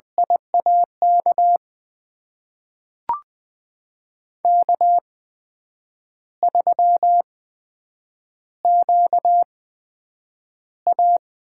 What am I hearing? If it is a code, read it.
Morse code, heard as a keyed tone, about 20 wpm, spaced as for 8 wpm, 700 Hz. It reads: IAK K3QA
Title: USA Call Signs - 8x Character Spacing - 20wpm